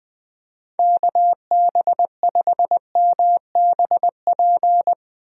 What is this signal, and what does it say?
Morse code, heard as a keyed tone, about 20 wpm, 700 Hz. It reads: KB5MBP
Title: USA Call Signs - 6 Characters - Repeated 2x - 20wpm